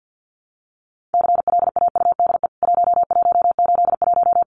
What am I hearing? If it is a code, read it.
Morse code, heard as a keyed tone, about 50 wpm, 700 Hz. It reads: CLAUDE 1181
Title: CWT Exchange 50wpm